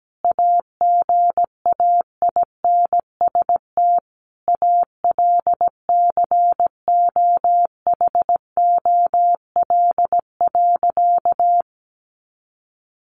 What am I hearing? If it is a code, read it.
Morse code, heard as a keyed tone, about 17 wpm, 700 Hz. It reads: AGAINST ALCOHOL.
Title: Sentences from Top 1400 Words 17wpm